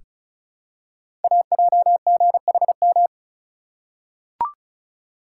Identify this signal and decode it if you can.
Morse code, heard as a keyed tone, about 36 wpm, 690 Hz. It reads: AJGHM
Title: Code Groups - 5 Letters - 35wpm